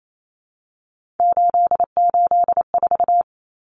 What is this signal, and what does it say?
Morse code, heard as a keyed tone, about 28 wpm, 700 Hz. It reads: 884
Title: Common Cut Numbers - 3-Digit 28wpm